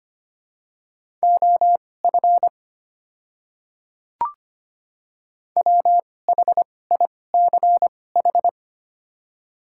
Morse code, heard as a keyed tone, about 25 wpm, spaced as for 18 wpm, 700 Hz. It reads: OF WHICH
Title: 2x Character Spacing - Top 100 Words - 25wpm